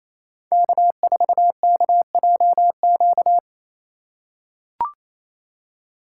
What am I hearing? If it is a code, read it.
Morse code, heard as a keyed tone, about 28 wpm, 700 Hz. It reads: K4KJQ
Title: USA Call Signs - 5 Characters - 2x Character Spacing - 28wpm